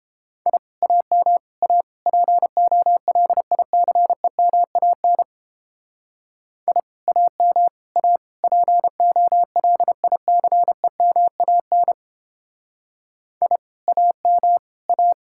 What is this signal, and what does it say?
Morse code, heard as a keyed tone, about 31 wpm, 700 Hz. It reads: I AM A POLICEMAN I AM A POLICEMAN I AM A